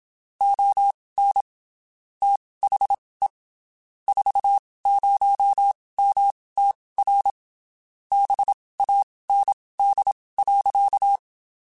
Morse code, heard as a keyed tone, about 27 wpm, spaced as for 19 wpm, 800 Hz. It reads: ON THE 40MTR BAND.
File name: (spaced) / QSOs Spaced-mp3s / kc7haa @25 sp